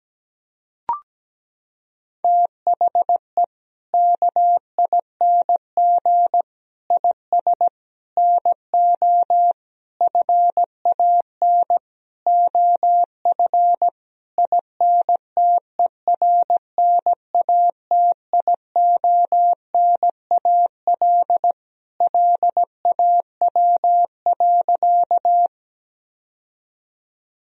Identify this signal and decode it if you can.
Morse code, heard as a keyed tone, about 17 wpm, 700 Hz. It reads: THE KING IS NO FAN OF INTERNATIONAL LAW.
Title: Sentences from Top 1000 Words - Review Part 4 - 17wpm